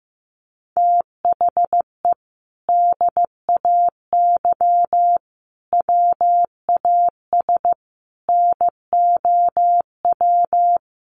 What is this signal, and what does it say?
Morse code, heard as a keyed tone, about 15 wpm, 700 Hz. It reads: THE DAY WAS NOW
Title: Sentences from Top 1200 Words 15wpm